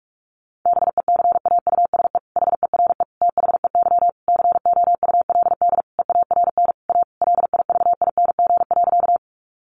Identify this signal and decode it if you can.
Morse code, heard as a keyed tone, about 44 wpm, 700 Hz. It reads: BECAUSE HERE THEY COULD EARN A LIVING.
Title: Sentences from Top 1500 Words 45wpm